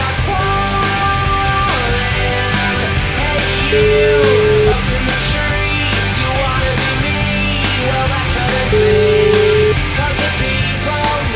Ringback tone (Europe, 425 Hz), with noise.